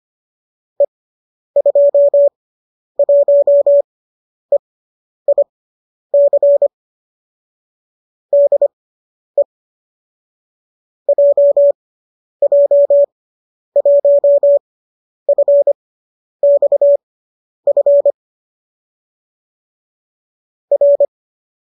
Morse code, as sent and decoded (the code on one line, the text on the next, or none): . ..--- .---- . .. -.-. / -.. / .--- .--- .---- ..-. -..- ..-. / .-.
E21EIC DE JJ1FXF R